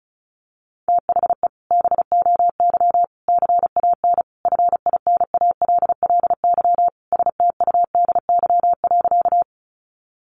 Morse code, heard as a keyed tone, about 34 wpm, 700 Hz. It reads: THE BOY CAN FINALLY STUDY.
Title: Additional Easy Phrases 35wpm